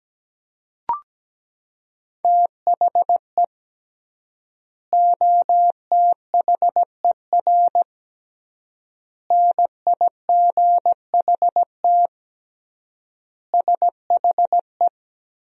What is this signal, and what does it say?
Morse code, heard as a keyed tone, about 17 wpm, 700 Hz. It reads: THE OTHER NIGHT SHE
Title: 3x Word-Spacing - Sentences from Top 300 Words - 17wpm